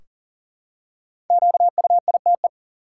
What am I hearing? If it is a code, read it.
Morse code, heard as a keyed tone, about 38 wpm, 700 Hz. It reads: QUITE
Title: Five Letter Words - 40wpm